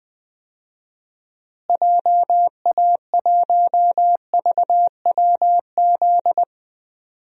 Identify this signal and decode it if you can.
Morse code, heard as a keyed tone, about 20 wpm, 700 Hz. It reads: JA1VWZ
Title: DXCC-100 Call Signs - 6 Characters - Repeated 3x - 20wpm